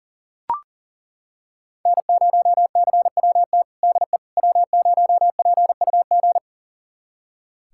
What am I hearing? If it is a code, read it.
Morse code, heard as a keyed tone, about 39 wpm, 700 Hz. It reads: N0CWT DE W0PUG
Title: USA Callsign DE Callsign - 5 Characters - 40wpm